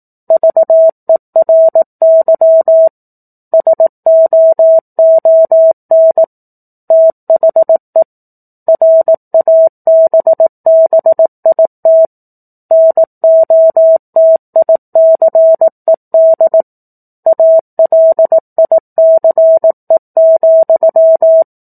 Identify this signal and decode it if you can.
Morse code, heard as a keyed tone, about 18 wpm, 650 Hz. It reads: VERY SOON THE RABBIT NOTICED ALICE,
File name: 04-Chapter-04 18 WPM